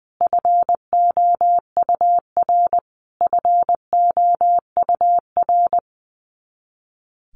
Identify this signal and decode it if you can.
Morse code, heard as a keyed tone, about 19 wpm, 700 Hz. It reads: FOUR FOUR
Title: Top 500 Words Repeated 20wpm